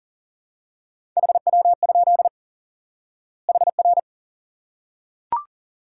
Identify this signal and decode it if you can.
Morse code, heard as a keyed tone, about 40 wpm, 700 Hz. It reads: HW? HR
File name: IQE Course 3 - Lesson 89 - YAGI - Two Elements - 40wpm